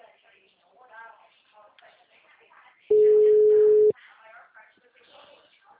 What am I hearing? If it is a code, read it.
Ringback tone (Europe, 425 Hz), with speech.
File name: out- -2008- - - .39798